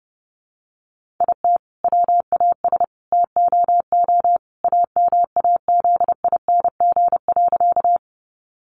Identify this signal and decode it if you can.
Morse code, heard as a keyed tone, about 30 wpm, 700 Hz. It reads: IT WAS TOO AMAZING.
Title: Sentences from Top 1500 Words 30wpm